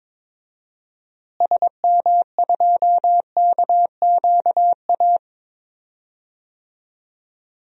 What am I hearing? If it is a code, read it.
Morse code, heard as a keyed tone, about 22 wpm, 700 Hz. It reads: SM2KQA
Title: DXCC-100 Call Signs - 6 Characters - Repeated 3x - 22wpm